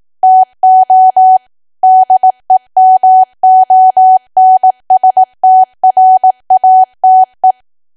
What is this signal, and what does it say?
Morse code, heard as a keyed tone, about 18 wpm, 750 Hz. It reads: TO DEMONSTRATE
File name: cwa / W1AW-Level2 / 18WPM